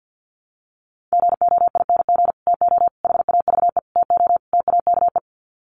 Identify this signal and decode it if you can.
Morse code, heard as a keyed tone, about 50 wpm, 700 Hz. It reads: GOING TO HAVE TO TAKE
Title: Sets of 5 Words Encore 50wpm